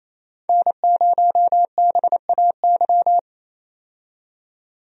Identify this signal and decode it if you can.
Morse code, heard as a keyed tone, about 28 wpm, 700 Hz. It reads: N0BAY
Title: USA Call Signs - 5 Characters - 28wpm